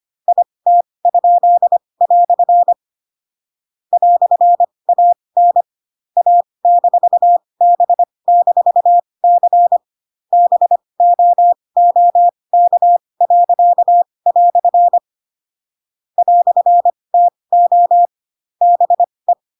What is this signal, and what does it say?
Morse code, heard as a keyed tone, about 25 wpm, spaced as for 20 wpm, 700 Hz. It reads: IT?" "AN A-B-C BOOK." "TO BE
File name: Pinocchio / Pinocchio ch8 15 25 ik1hge